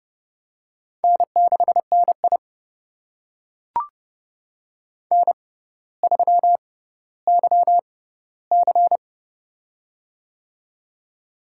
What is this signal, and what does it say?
Morse code, heard as a keyed tone, about 31 wpm, spaced as for 15 wpm, 700 Hz. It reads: N6NI N3YC
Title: USA Call Signs - 4 Characters - 6x Character Spacing - 30wpm